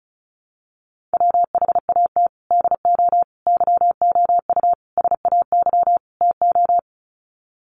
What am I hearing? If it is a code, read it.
Morse code, heard as a keyed tone, about 34 wpm, 700 Hz. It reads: WHAT DO YOU SAY TO
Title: Sets of 5 Words 35wpm